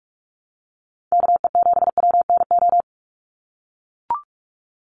Morse code, heard as a keyed tone, about 44 wpm, 700 Hz. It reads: KE7WNO
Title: USA Call Signs 45wpm